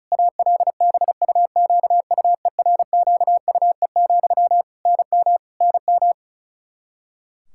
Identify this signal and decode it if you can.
Morse code, heard as a keyed tone, about 34 wpm, 700 Hz. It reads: ALBUQUERQUE, NM NM
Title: Practice QSO - Exchange 1 - Beginning - 35wpm